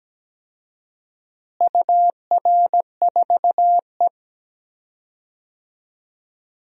Morse code, heard as a keyed tone, about 17 wpm, 700 Hz. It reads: UR4E
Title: DXCC-100 Call Signs - 4 Characters - Repeated 2x - 17wpm